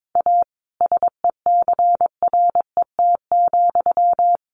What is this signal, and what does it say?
Morse code, heard as a keyed tone, about 22 wpm, 700 Hz. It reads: A SECRET,
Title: Sentences from Top 1200 Words 22wpm